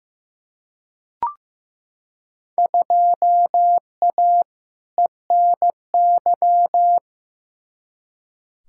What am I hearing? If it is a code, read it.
Morse code, heard as a keyed tone, about 15 wpm, 700 Hz. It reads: 2A ENY
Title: Field Day Exchanges Encore - 15wpm